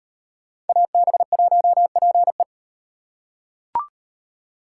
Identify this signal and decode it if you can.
Morse code, heard as a keyed tone, about 38 wpm, 700 Hz. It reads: AB1PE